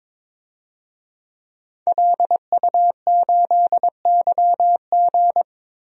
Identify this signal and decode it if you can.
Morse code, heard as a keyed tone, about 22 wpm, 700 Hz. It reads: LU8YG
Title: DXCC-100 Call Signs - 5 Characters - Repeated 3x - 22wpm